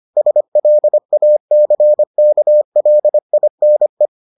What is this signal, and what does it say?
Morse code, heard as a keyed tone, about 25 wpm, 600 Hz. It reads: SLACKLINE